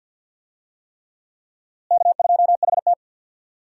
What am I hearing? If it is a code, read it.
Morse code, heard as a keyed tone, about 46 wpm, 700 Hz. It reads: K2HT
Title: USA Call Signs - 4 Characters - Repeated 3x - 50wpm